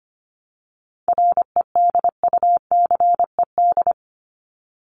Morse code, heard as a keyed tone, about 25 wpm, 700 Hz. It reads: REDUCED
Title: ED Suffix - Top 100 Words with Base Words 25wpm